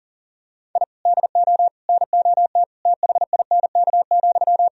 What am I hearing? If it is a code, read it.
Morse code, heard as a keyed tone, about 40 wpm, 700 Hz. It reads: I DO NOT THINK,
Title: Sentences from Top 1000 Words - Review Part 3 - 40wpm